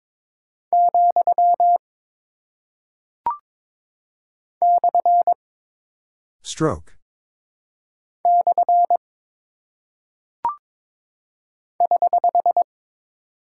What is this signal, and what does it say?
Morse code, heard as a keyed tone, about 22 wpm, 700 Hz. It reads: , / / <HH>
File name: practice / 22wpm / Single Punctuation-Prosign - 22wpm